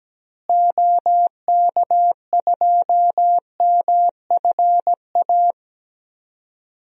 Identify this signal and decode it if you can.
Morse code, heard as a keyed tone, about 17 wpm, 700 Hz. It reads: OK2MFA
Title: DXCC-100 Call Signs - 6 Characters - Repeated 3x - 17wpm